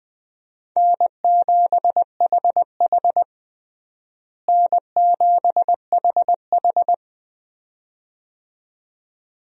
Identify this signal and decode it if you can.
Morse code, heard as a keyed tone, about 20 wpm, 700 Hz. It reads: N7HH N7HH
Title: USA Call Signs - 4 Characters - Repeated 2x - 20wpm